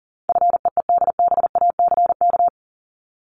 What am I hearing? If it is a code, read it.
Morse code, heard as a keyed tone, about 40 wpm, 700 Hz. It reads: FEEDBACK
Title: Top 1501-1750 Words 40wpm